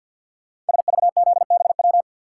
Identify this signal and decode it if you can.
Morse code, heard as a keyed tone, about 50 wpm, 690 Hz. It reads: SVZBW
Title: Code Groups - 5 Letters - 50wpm